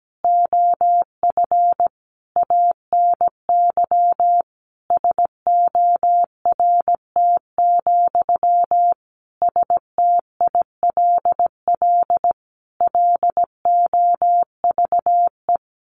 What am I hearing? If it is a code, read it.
Morse code, heard as a keyed tone, about 17 wpm, 700 Hz. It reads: OF ANY SORT, STILL LOVE
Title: Sentences from Top 1500 Words 17wpm